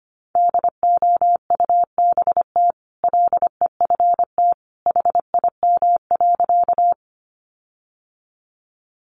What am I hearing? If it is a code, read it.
Morse code, heard as a keyed tone, about 25 wpm, 700 Hz. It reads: DOUBT LEFT HIM.
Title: Sentences from Top 1500 Words 25wpm